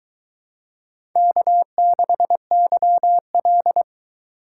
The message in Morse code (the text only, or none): K6YL